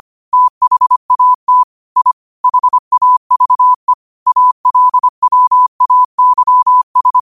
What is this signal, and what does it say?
Morse code, heard as a keyed tone, about 25 wpm, 1000 Hz. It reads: THAT I HAVE ALWAYS
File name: qotdfiles / 25wpm